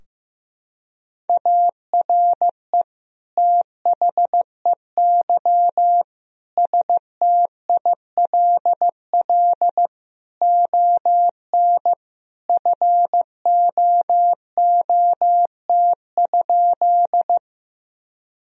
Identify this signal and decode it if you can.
Morse code, heard as a keyed tone, about 15 wpm, 700 Hz. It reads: ARE THEY STILL ON FOOT?
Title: Sentences from Top 1000 Words - Review Part 3 - 15wpm